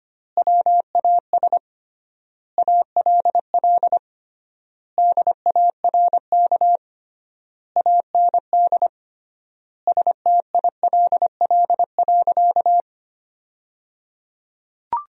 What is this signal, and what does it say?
Morse code, heard as a keyed tone, about 25 wpm, 700 Hz. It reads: WAS ALL DARK AND STILL.
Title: 3x Word-Spacing - Sentences from Top 500 Words - 25wpm